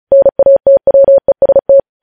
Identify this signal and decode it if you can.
Morse code, heard as a keyed tone, about 36 wpm, 550 Hz. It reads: NATWEST
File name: podcast / News headlines-1x-35x35